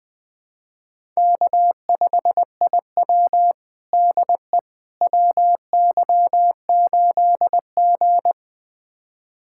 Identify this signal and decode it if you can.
Morse code, heard as a keyed tone, about 20 wpm, 700 Hz. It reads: K5IW DE WY8G
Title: USA Callsign DE Callsign - 4 Characters - 20wpm